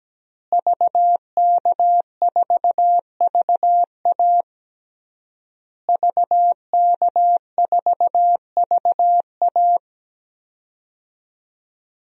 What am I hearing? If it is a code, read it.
Morse code, heard as a keyed tone, about 17 wpm, 700 Hz. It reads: VK4VA VK4VA
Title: DXCC-100 Call Signs - 5 Characters - Repeated 3x - 17wpm